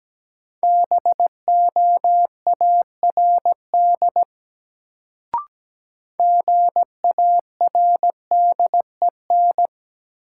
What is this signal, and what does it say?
Morse code, heard as a keyed tone, about 17 wpm, 700 Hz. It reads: BOARD GARDEN
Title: Sentences from Top 700 Words - New Word Drill - 17wpm